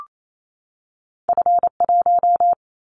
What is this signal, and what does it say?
Morse code, heard as a keyed tone, about 28 wpm, 700 Hz. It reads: F1
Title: Callsign Prefixes for Top 50 DXCC - 28wpm